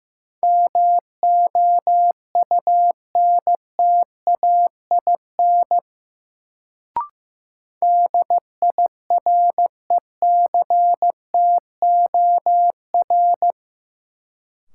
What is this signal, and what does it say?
Morse code, heard as a keyed tone, about 15 wpm, 700 Hz. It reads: MOUNTAIN DIRECTOR
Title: Sentences from Top 1300 Words - New Word Drill Alt - 15wpm